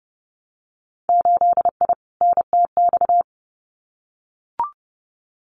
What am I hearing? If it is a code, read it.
Morse code, heard as a keyed tone, about 30 wpm, 700 Hz. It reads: 8I NTX
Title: Winter Field Day Exchanges Encore 30wpm